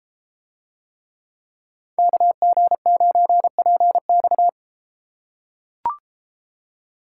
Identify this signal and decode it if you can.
Morse code, heard as a keyed tone, about 30 wpm, 700 Hz. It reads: KG9PX